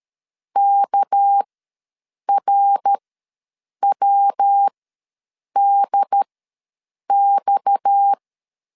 Morse code, heard as a keyed tone, about 13 wpm, spaced as for 7 wpm, 800 Hz. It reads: KRWDX
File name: LEZIONI / 6 PKX0 / 6 PKX ws 12wpm 3106